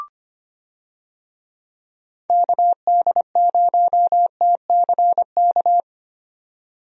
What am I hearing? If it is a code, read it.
Morse code, heard as a keyed tone, about 25 wpm, 700 Hz. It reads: KD0TCK
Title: USA Call Signs - 6 Characters - 25wpm